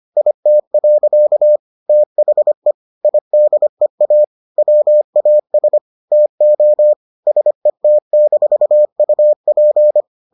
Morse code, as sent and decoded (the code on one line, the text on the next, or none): .. - .-.-.- / - .... / .. -.. .- / .-- .- ... / - --- / ... - -....- ..- .--.
IT. THE IDEA WAS TO SET-UP